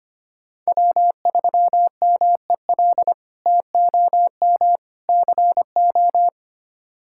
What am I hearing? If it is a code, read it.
Morse code, heard as a keyed tone, about 25 wpm, 700 Hz. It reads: W3MEL TOM CO
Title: K1USN SST QSO Practice - 25wpm